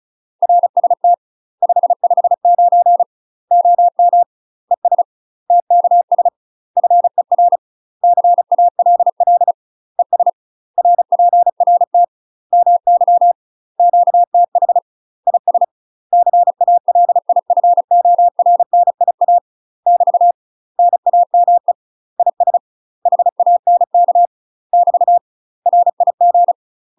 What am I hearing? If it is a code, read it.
Morse code, heard as a keyed tone, about 34 wpm, 700 Hz. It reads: RST 559 OM ES TKS FER CALL ES RPRT MY QTH IS CALIFORNIA <BT> NAME IS HANK <BT> RIG
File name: UK MASTER QSOs TRAINING / 35WPM / UK MASTER QSOs 0040